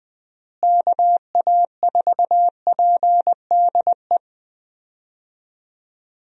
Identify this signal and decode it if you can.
Morse code, heard as a keyed tone, about 20 wpm, 700 Hz. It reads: KA4PDE
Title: USA Call Signs - 6 Characters - Repeated 2x - 20wpm